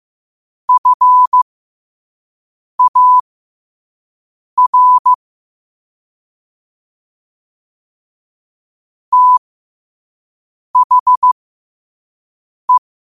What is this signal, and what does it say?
Morse code, heard as a keyed tone, about 15 wpm, spaced as for 5 wpm, 1000 Hz. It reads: FAR THE
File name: qotdfiles / 5wpm